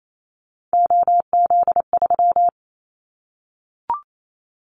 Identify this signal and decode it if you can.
Morse code, heard as a keyed tone, about 28 wpm, 700 Hz. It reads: OZ3
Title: Callsign Prefixes for Top 100 DXCC - 28wpm